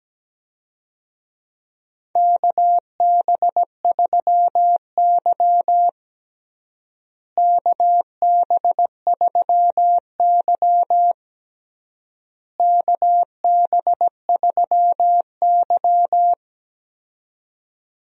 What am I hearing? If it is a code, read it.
Morse code, heard as a keyed tone, about 17 wpm, 700 Hz. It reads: KB3Y KB3Y KB3Y
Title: USA Call Signs - 4 Characters - Repeated 3x - 17wpm